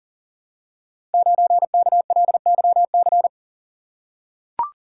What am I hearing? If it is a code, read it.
Morse code, heard as a keyed tone, about 38 wpm, 700 Hz. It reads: 9KLYC
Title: Code Groups - 5 Letter:Numbers - 40wpm